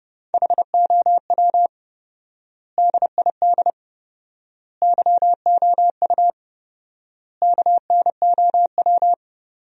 Morse code, heard as a keyed tone, about 30 wpm, 700 Hz. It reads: HOW DID YOU KNOW
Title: 4x Word-Spacing - Sentences from Top 100 Words - 30wpm